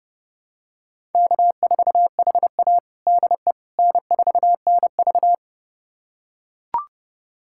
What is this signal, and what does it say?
Morse code, heard as a keyed tone, about 30 wpm, 700 Hz. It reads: K4HA DE N4NV
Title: USA Callsign DE Callsign - 4 Characters - 30wpm